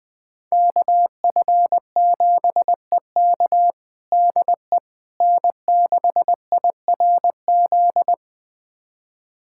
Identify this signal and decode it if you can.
Morse code, heard as a keyed tone, about 21 wpm, 700 Hz. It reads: KF7EK DE N6IRZ
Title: USA Callsign DE Callsign - 5 Characters - 20wpm